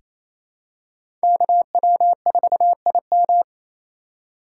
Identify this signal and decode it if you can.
Morse code, heard as a keyed tone, about 27 wpm, 700 Hz. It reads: KW4IM
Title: USA Call Signs - 3x Character Spacing - 28wpm